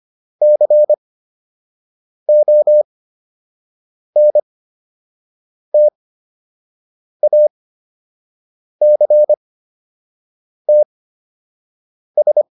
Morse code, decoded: CONTACTS

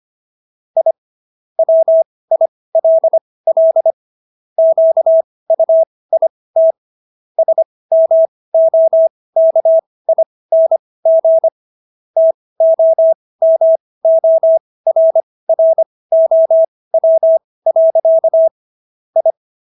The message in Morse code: I WILL QUIT SMOKING TOMORROW. I

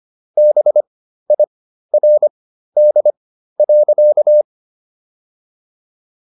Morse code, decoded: BIRD.

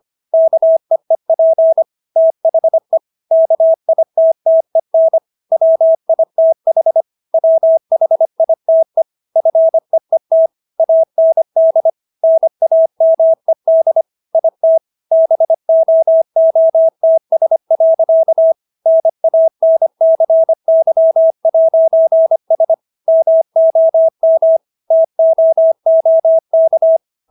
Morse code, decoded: KEEP THE KITTEN WITH WHITE FEET AND NAMED IT BOOTS. NANCY'S MOM TOOK

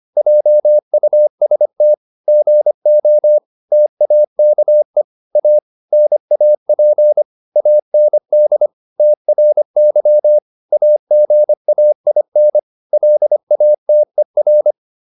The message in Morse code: JUST GO TAKE A NAP AND TRY AGAIN LATER

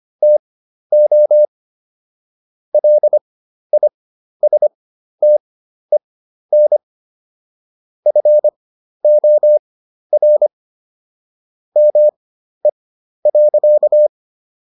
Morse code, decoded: TO LISTEN FOR ME.

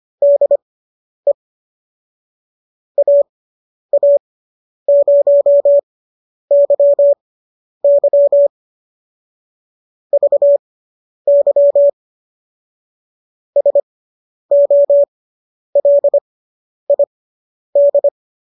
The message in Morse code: DE AA0YY VY SOLID

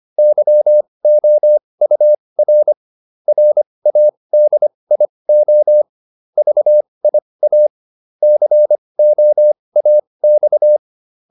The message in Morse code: YOUR RADIO VIA COAX

